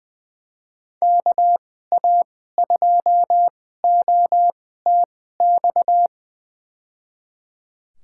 Morse code, heard as a keyed tone, about 20 wpm, spaced as for 14 wpm, 700 Hz. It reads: KA2OTX
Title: USA Call Signs - 2x Character Spacing - 20wpm